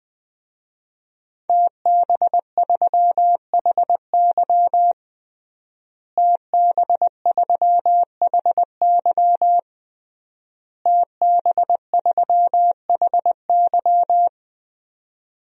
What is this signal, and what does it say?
Morse code, heard as a keyed tone, about 20 wpm, 700 Hz. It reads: TB3HY TB3HY TB3HY
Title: DXCC-100 Call Signs - 5 Characters - Repeated 3x - 20wpm